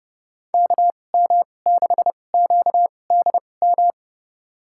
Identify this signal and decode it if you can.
Morse code, heard as a keyed tone, about 30 wpm, spaced as for 22 wpm, 700 Hz. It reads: KM6QDM
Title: USA Call Signs - 2x Character Spacing - 30wpm